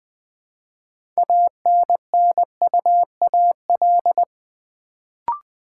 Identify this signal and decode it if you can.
Morse code, heard as a keyed tone, about 20 wpm, 700 Hz. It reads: ANNUAL
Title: Build A Word - 3 Letters At A Time - 6-Letter Words - 20wpm